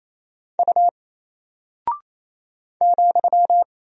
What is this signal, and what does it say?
Morse code, heard as a keyed tone, about 27 wpm, 700 Hz. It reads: U ,
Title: Single Character (Letter:Number:Punctuation:Prosign) 28wpm